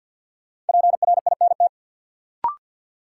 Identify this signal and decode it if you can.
Morse code, heard as a keyed tone, about 47 wpm, 700 Hz. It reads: PRINT
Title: Sentences from Top 1400 Words - New Word Drill - 50wpm